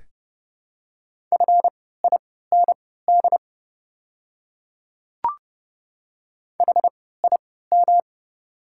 Morse code, heard as a keyed tone, about 31 wpm, spaced as for 17 wpm, 700 Hz. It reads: FIND HIM
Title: 3x Character Spacing - Top 500 Words - 30wpm